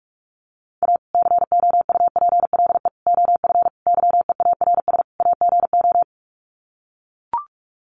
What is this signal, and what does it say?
Morse code, heard as a keyed tone, about 44 wpm, 700 Hz. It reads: A COUPLE OF YEARS AGO
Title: Sets of 5 Words 45wpm